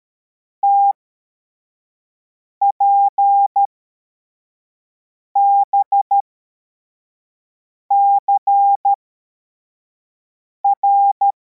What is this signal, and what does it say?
Morse code, heard as a keyed tone, about 13 wpm, spaced as for 4 wpm, 800 Hz. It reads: TPBCR